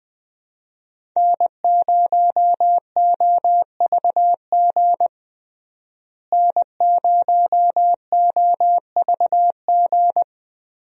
Morse code, heard as a keyed tone, about 20 wpm, 700 Hz. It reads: N0OVG N0OVG
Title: USA Call Signs - 5 Characters - Repeated 3x - 20wpm